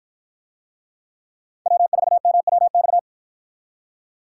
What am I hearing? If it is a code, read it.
Morse code, heard as a keyed tone, about 53 wpm, 700 Hz. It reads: W4MWX